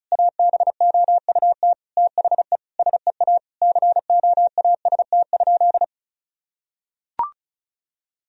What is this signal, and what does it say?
Morse code, heard as a keyed tone, about 36 wpm, 700 Hz. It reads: ABOUT THE SEA COAST?
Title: Sentences from Top 1000 Words - Review Part 4 - 35wpm